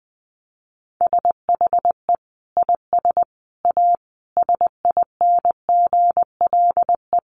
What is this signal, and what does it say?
Morse code, heard as a keyed tone, about 20 wpm, 700 Hz. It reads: SHE IS A SINGLE
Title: Sentences from Top 700 Words 20wpm